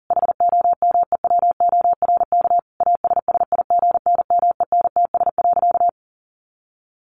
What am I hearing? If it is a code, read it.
Morse code, heard as a keyed tone, about 37 wpm, 700 Hz. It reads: HOMEWORK ASSIGNMENTS.
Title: Easy Phrases 40wpm